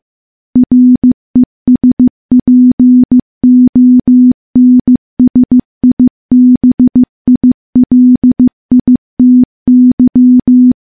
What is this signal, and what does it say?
Morse code, heard as a keyed tone, about 15 wpm, 250 Hz. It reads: RESPONSIBILITY
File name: morse-training-15WPM